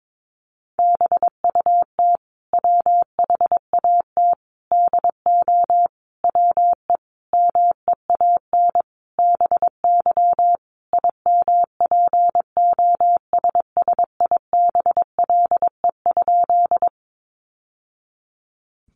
Morse code, heard as a keyed tone, about 22 wpm, 700 Hz. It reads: BUT WHAT DO WE MEAN BY IMPOSSIBLE?
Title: Sentences from Top 1100 Words 22wpm